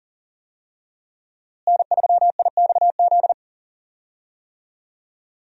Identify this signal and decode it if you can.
Morse code, heard as a keyed tone, about 40 wpm, 700 Hz. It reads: N3IXZ
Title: USA Call Signs - 5 Characters - Repeated 2x - 40wpm